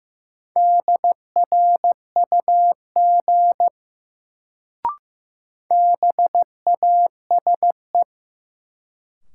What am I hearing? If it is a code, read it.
Morse code, heard as a keyed tone, about 15 wpm, 700 Hz. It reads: DRUG BASE
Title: Sets of 5 Words - New Word Drill - 15wpm